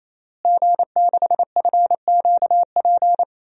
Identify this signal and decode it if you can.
Morse code, heard as a keyed tone, about 28 wpm, 700 Hz. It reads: G6FQP
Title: DXCC-100 Call Signs - 5 Characters - Repeated 3x - 28wpm